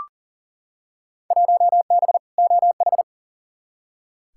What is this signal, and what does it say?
Morse code, heard as a keyed tone, about 39 wpm, 700 Hz. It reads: 1B OH
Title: Field Day Exchanges - 40wpm